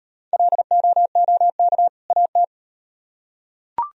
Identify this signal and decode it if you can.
Morse code, heard as a keyed tone, about 38 wpm, 700 Hz. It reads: LOOK AT